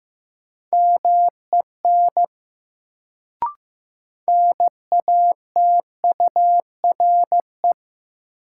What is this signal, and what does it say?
Morse code, heard as a keyed tone, about 15 wpm, 700 Hz. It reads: MEN NATURE
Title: Sets of 3 Words - New Word Drill - 15wpm